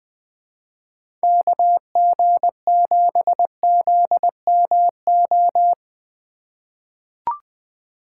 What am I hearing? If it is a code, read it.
Morse code, heard as a keyed tone, about 20 wpm, 700 Hz. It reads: KG7ZMO
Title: USA Call Signs - 2x Character Spacing - 20wpm